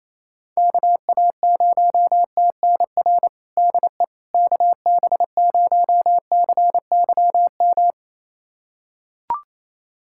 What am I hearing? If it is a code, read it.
Morse code, heard as a keyed tone, about 28 wpm, 700 Hz. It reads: KA0TNR DE KB0CYM
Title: USA Callsign DE Callsign 28wpm